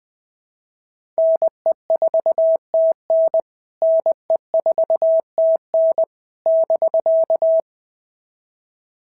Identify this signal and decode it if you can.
Morse code, heard as a keyed tone, about 20 wpm, 660 Hz. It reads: NE4TN NE4TN <BK>
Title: POTA QSO Practice - 20wpm